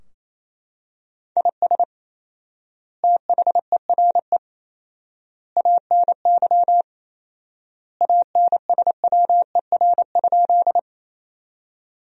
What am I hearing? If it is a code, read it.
Morse code, heard as a keyed tone, about 28 wpm, 700 Hz. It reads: IS THERE ANY ANSWER?
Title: 4x Word-Spacing - Sentences from Top 300 Words - 28wpm